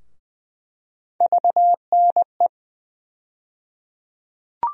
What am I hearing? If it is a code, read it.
Morse code, heard as a keyed tone, about 20 wpm, 700 Hz. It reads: VNE
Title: USA Call Sign Postfixes - 3 Letters 20wpm